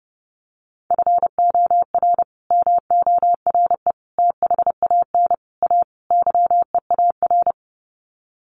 Morse code, heard as a keyed tone, about 30 wpm, 700 Hz. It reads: FOR MORE THAN A YEAR
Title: Sets of 5 Words 30wpm